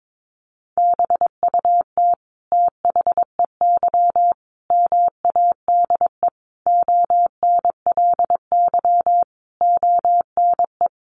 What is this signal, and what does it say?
Morse code, heard as a keyed tone, about 22 wpm, 700 Hz. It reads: BUT THEY MADE ONLY ONE
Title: Sentences from Top 1300 Words 22wpm